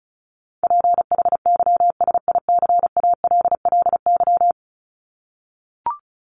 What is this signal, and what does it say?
Morse code, heard as a keyed tone, about 34 wpm, 700 Hz. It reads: PHYSICALLY
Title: Top 1501-1750 Words 35wpm